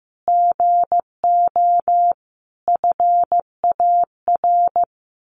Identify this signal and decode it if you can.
Morse code, heard as a keyed tone, about 15 wpm, 700 Hz. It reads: GO FAR
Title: Sentences from Top 1500 Words 15wpm